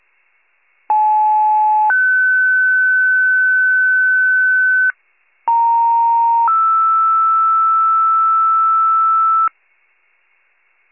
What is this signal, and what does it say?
Two-tone page, heard as a one-second tone, then a three-second tone, about 860 then 1500 Hz, about 930 then 1320 Hz. It is heard from 0.9 to 9.5 s.
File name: pager / 75 FIRE / wcalert